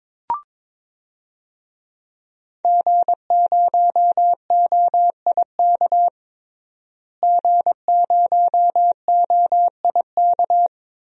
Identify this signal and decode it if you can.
Morse code, heard as a keyed tone, about 22 wpm, 700 Hz. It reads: G0OIK G0OIK
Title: DXCC-100 Call Signs - 5 Characters - Repeated 2x - 22wpm